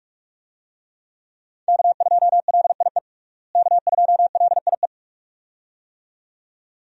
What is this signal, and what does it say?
Morse code, heard as a keyed tone, about 47 wpm, 700 Hz. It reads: K2LIE K2LIE
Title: USA Call Signs - 5 Characters - Repeated 2x - 45wpm